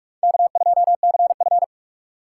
Morse code, heard as a keyed tone, about 45 wpm, 700 Hz. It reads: K2CF
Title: USA Call Signs - 4 Characters - 4x Character Spacing - 45wpm